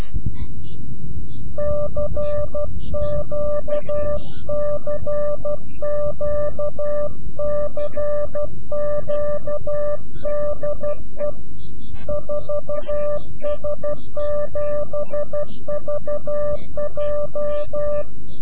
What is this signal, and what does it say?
Morse code, heard as a keyed tone, about 12 wpm, 590 Hz. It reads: CQ CQ CQ DE 4S7VJ
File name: CW-QSO / CQ Call / CQ-QSO